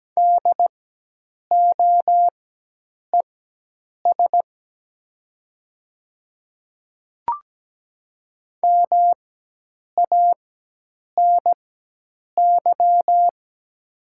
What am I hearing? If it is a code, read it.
Morse code, heard as a keyed tone, about 17 wpm, spaced as for 8 wpm, 700 Hz. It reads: DOES MANY
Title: 4x Character Spacing - Top 200 Words - 17wpm